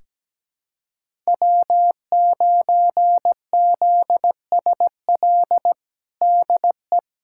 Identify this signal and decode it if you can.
Morse code, heard as a keyed tone, about 17 wpm, 700 Hz. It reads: W9ZSL DE
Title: USA Callsign DE Callsign - 5 Characters - 17wpm